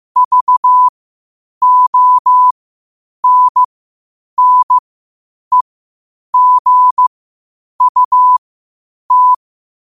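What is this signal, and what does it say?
Morse code, heard as a keyed tone, about 15 wpm, spaced as for 8 wpm, 1000 Hz. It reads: VONNEGUT
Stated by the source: Morse code quotes at 8 WPM.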